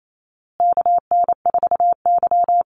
Morse code, heard as a keyed tone, about 28 wpm, 700 Hz. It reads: KN4Y
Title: Call Signs Repeated 2x - 28wpm